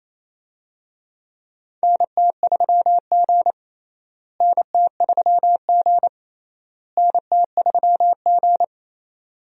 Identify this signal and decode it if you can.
Morse code, heard as a keyed tone, about 28 wpm, 700 Hz. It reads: NT3G NT3G NT3G
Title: USA Call Signs - 4 Characters - Repeated 3x - 28wpm